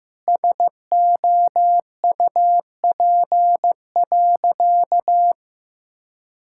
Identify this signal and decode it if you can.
Morse code, heard as a keyed tone, about 15 wpm, 700 Hz. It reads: SOUP.
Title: Sentences from Top 1000 Words - Review Part 6 - 15wpm